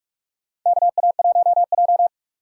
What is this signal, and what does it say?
Morse code, heard as a keyed tone, about 45 wpm, 700 Hz. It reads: KA1J
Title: USA Call Signs - 4 Characters - 2x Character Spacing - 45wpm